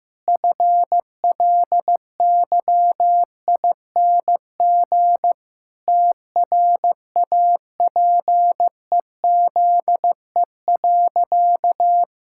Morse code, heard as a keyed tone, about 15 wpm, 700 Hz. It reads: FLYING TRAPEZE.
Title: Pangram Challenge v2 - 15wpm